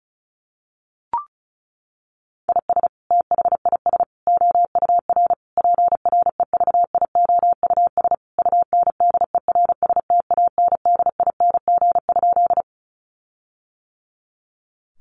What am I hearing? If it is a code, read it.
Morse code, heard as a keyed tone, about 34 wpm, 700 Hz. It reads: IS THIS OUR PREVIOUS UNDERSTANDING?
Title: Sentences from Top 1500 Words 35wpm